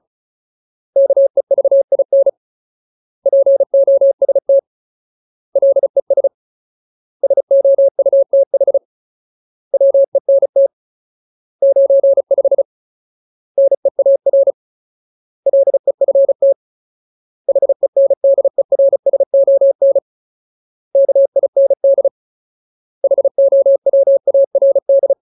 Morse code, heard as a keyed tone, about 34 wpm, 550 Hz. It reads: KEVIN POST LES SOUTH WENT 95 NEAR LEFT HENDERSON KIND HOWARD
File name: Ultimate Word List / 35WPM Ultimate